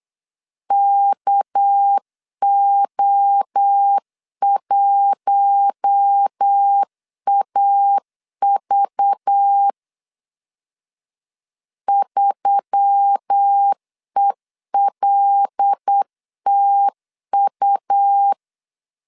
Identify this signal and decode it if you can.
Morse code, heard as a keyed tone, about 8 wpm, 800 Hz. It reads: KO1AV 3ELTU
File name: LEZIONI / 7 GF9Z / 7 GF9 8wpm 3545